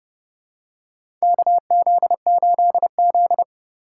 Morse code, heard as a keyed tone, about 32 wpm, 700 Hz. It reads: KZ8Z
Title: USA Call Signs - 4 Characters - 8x Character Spacing - 30wpm